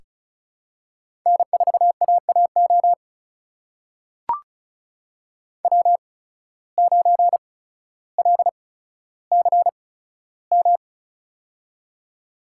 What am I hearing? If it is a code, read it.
Morse code, heard as a keyed tone, about 36 wpm, spaced as for 15 wpm, 700 Hz. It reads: N4AAO W9LCM
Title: USA Call Signs - 5 Characters - 8x Character Spacing - 35wpm